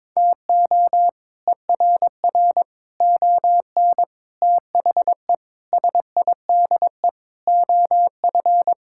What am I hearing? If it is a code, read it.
Morse code, heard as a keyed tone, about 22 wpm, 700 Hz. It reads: TO ERR ON THE SIDE OF
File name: practice / 22wpm / Sets of 7 Words - 22wpm